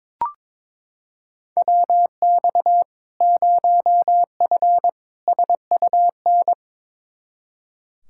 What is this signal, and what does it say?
Morse code, heard as a keyed tone, about 22 wpm, 700 Hz. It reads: WX 0F SUN
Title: Practice QSO - Exchange 2 - Info - 22wpm